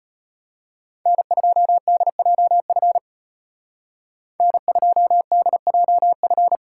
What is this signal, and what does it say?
Morse code, heard as a keyed tone, about 33 wpm, 700 Hz. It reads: N2DJF N2DJF